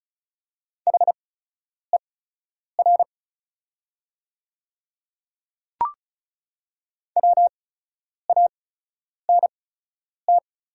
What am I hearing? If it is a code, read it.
Morse code, heard as a keyed tone, about 36 wpm, spaced as for 10 wpm, 700 Hz. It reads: HER WANT
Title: 8x Character Spacing - Top 200 Words - 35wpm